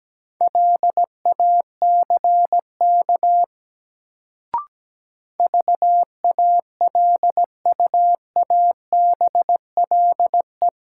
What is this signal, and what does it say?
Morse code, heard as a keyed tone, about 17 wpm, 700 Hz. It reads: LACK VALUABLE
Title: Sentences from Top 1100 Words - New Word Drill - 17wpm